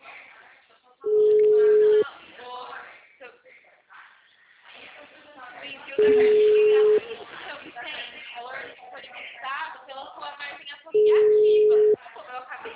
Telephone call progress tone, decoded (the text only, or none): ringback tone